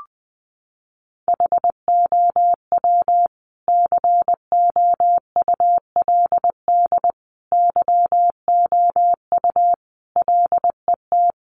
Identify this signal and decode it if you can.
Morse code, heard as a keyed tone, about 20 wpm, 700 Hz. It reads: HOW COULD YOU LET
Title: Sentences from Top 300 Words 20wpm